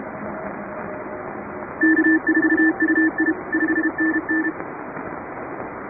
Morse code, heard as a keyed tone, about 32 wpm, 1750 Hz. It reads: K4UI 5NN
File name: chatham / 2350-12CW